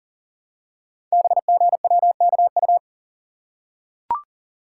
Morse code, heard as a keyed tone, about 39 wpm, 700 Hz. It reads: BGWKU
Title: Code Groups - 5 Letters - 40wpm